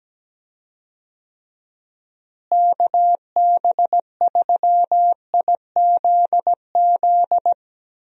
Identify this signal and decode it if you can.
Morse code, heard as a keyed tone, about 17 wpm, 700 Hz. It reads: KB3IZZ
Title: USA Call Signs - 6 Characters - Repeated 2x - 17wpm